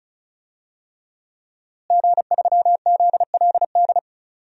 Morse code, heard as a keyed tone, about 34 wpm, 700 Hz. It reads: G3ZLD
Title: DXCC-100 Call Signs - 5 Characters - Repeated 3x - 35wpm